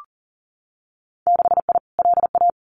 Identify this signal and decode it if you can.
Morse code, heard as a keyed tone, about 40 wpm, 700 Hz. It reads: 6I LA
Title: Winter Field Day Exchanges Encore 40wpm